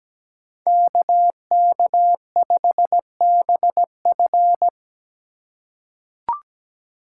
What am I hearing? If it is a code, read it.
Morse code, heard as a keyed tone, about 17 wpm, 700 Hz. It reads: KK5BF
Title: USA Call Signs - 5 Characters - 4x Character Spacing - 17wpm